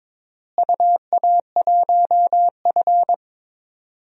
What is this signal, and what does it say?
Morse code, heard as a keyed tone, about 22 wpm, 700 Hz. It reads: UA1F
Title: DXCC-100 Call Signs - 4 Characters - Repeated 2x - 22wpm